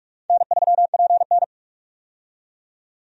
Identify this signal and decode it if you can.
Morse code, heard as a keyed tone, about 46 wpm, 700 Hz. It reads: N3PN
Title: USA Call Signs - 4 Characters - 45wpm